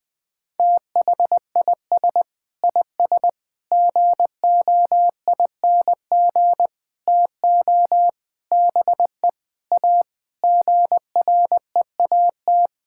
Morse code, heard as a keyed tone, about 20 wpm, 700 Hz. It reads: THIS IS GOING TO BE A GREAT
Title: Sentences with 6-Word Phrases - Spoken Beginning - 20wpm